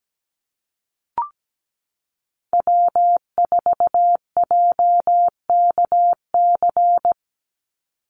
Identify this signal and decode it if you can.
Morse code, heard as a keyed tone, about 17 wpm, 700 Hz. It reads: W4JKC
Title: NA SOTA Chaser Callsigns 17wpm